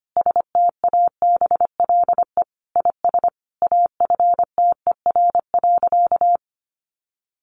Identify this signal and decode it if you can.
Morse code, heard as a keyed tone, about 25 wpm, 700 Hz. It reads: STABLE IS AFTER.
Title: Sentences from Top 1500 Words 25wpm